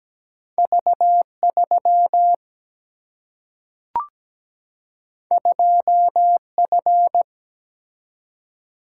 Morse code, heard as a keyed tone, about 17 wpm, 700 Hz. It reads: V3 2F
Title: Single-Letter Single-Number 17wpm